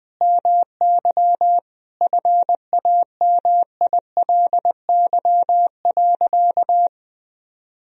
Morse code, heard as a keyed tone, about 21 wpm, 700 Hz. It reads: MY FAMILY.
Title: Sentences from Top 1000 Words - Review Part 2 - 20wpm